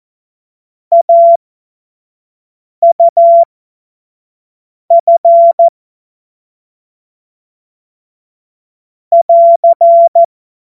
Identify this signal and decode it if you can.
Morse code, heard as a keyed tone, about 14 wpm, spaced as for 5 wpm, 680 Hz. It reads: AUF <AR>
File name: Morsekurs / Witze / Witz 09. Dezember 2025 25 F70